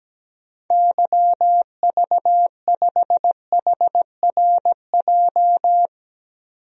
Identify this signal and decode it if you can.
Morse code, heard as a keyed tone, about 17 wpm, 700 Hz. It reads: YV5HRJ
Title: DXCC-100 Call Signs - 6 Characters - 17wpm